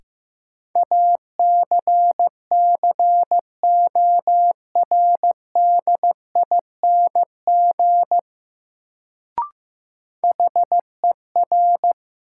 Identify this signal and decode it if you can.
Morse code, heard as a keyed tone, about 15 wpm, 700 Hz. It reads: ACCORDING HER
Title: Sets of 2 Words - New Word Drill - 15wpm